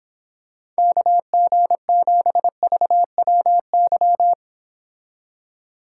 Morse code, heard as a keyed tone, about 26 wpm, 700 Hz. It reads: KG7VWY